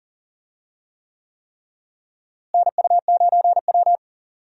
Morse code, heard as a keyed tone, about 38 wpm, 700 Hz. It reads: NU9W
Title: USA Call Signs - 4 Characters - Repeated 3x - 40wpm